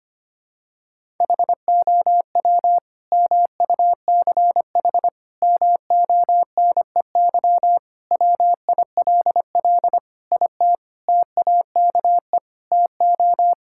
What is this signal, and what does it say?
Morse code, heard as a keyed tone, about 25 wpm, 700 Hz. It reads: HOW MUCH MONEY WILL IT TAKE TO
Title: Sentences from Top 1000 Words - Review Part 5 - 25wpm